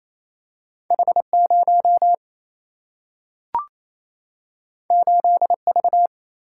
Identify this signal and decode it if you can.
Morse code, heard as a keyed tone, about 28 wpm, 700 Hz. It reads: H0 8V
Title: Single-Letter Single-Number 28wpm